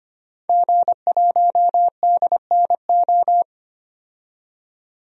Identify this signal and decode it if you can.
Morse code, heard as a keyed tone, about 25 wpm, 700 Hz. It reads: G1DNO
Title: DXCC-100 Call Signs - 5 Characters - 25wpm